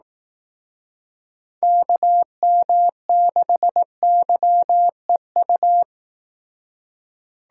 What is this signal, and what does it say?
Morse code, heard as a keyed tone, about 18 wpm, 700 Hz. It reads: KM6YEU
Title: Speed Racing - USA Call Signs 15wpm